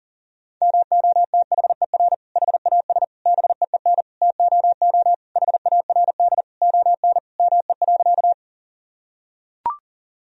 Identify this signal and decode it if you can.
Morse code, heard as a keyed tone, about 40 wpm, 700 Hz. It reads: MOTHER HAS BEEN TOO HARD ON ME.
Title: Sentences from Top 1000 Words - Review Part 1 - 40wpm